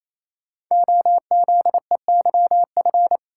Morse code, heard as a keyed tone, about 27 wpm, 700 Hz. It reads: OZEYF
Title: Code Groups - 5 Letter:Numbers - 28wpm